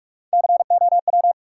Morse code, heard as a keyed tone, about 47 wpm, 700 Hz. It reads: COW
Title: Three Letter Words - 45wpm